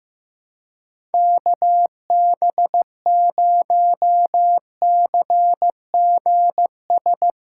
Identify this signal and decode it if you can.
Morse code, heard as a keyed tone, about 15 wpm, 700 Hz. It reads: KB0CGS
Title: USA Callsign DE Callsign - 6 Characters - 15wpm